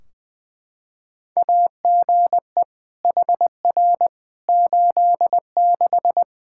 Morse code, heard as a keyed tone, about 21 wpm, 700 Hz. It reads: AGE HR 86
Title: Practice QSO - Exchange 3 - More Info - 20wpm